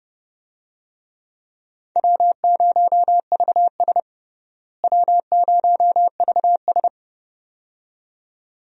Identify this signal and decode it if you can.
Morse code, heard as a keyed tone, about 30 wpm, 700 Hz. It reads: W0VS W0VS
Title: USA Call Signs - 4 Characters - Repeated 2x - 30wpm